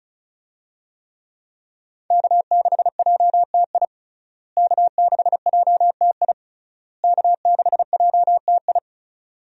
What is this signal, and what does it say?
Morse code, heard as a keyed tone, about 36 wpm, 700 Hz. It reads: K6JTI K6JTI K6JTI
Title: USA Call Signs - 5 Characters - Repeated 3x - 35wpm